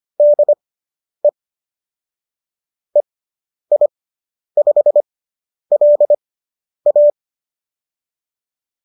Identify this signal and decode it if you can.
Morse code, heard as a keyed tone, about 25 wpm, spaced as for 10 wpm, 590 Hz. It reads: DE EI5LA